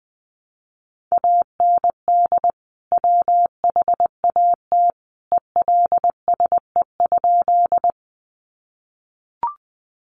Morse code, heard as a keyed tone, about 20 wpm, 700 Hz. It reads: AND WHAT ELSE?
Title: Sentences from Top 800 Words 20wpm